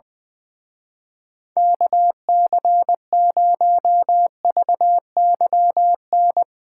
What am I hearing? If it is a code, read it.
Morse code, heard as a keyed tone, about 22 wpm, 700 Hz. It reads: KC0VYN
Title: Speed Racing - USA 2x3 Call Signs 20wpm